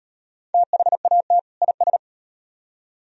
Morse code, heard as a keyed tone, about 38 wpm, 700 Hz. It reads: THAT IS